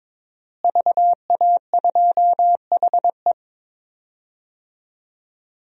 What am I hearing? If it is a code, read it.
Morse code, heard as a keyed tone, about 21 wpm, 700 Hz. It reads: VA2HE
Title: DXCC-100 Call Signs - 5 Characters - Repeated 2x - 22wpm